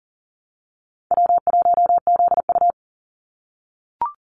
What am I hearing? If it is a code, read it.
Morse code, heard as a keyed tone, about 40 wpm, 700 Hz. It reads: W1ZU
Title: NA SOTA Activator Callsigns 40wpm